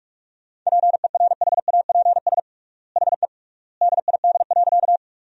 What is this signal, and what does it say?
Morse code, heard as a keyed tone, about 47 wpm, 700 Hz. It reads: PERHAPS HE DID.
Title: 3x Word-Spacing - Sentences from Top 500 Words - 45wpm